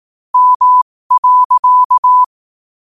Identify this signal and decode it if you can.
Morse code, heard as a keyed tone, about 18 wpm, spaced as for 15 wpm, 1000 Hz. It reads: M.
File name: qotdfiles / 15wpm